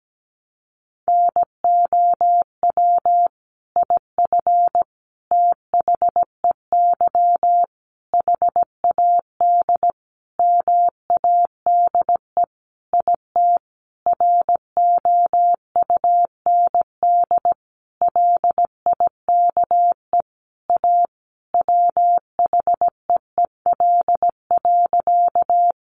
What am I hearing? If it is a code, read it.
Morse code, heard as a keyed tone, about 17 wpm, 700 Hz. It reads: NOW IF THEY HAD MADE IT ROUND LIKE A WHEEL.
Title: Sentences from Top 500 Words 17wpm